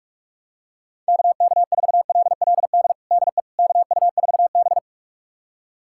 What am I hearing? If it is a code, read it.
Morse code, heard as a keyed tone, about 46 wpm, 700 Hz. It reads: KK4LLD DE KU4B
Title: USA Callsign DE Callsign 45wpm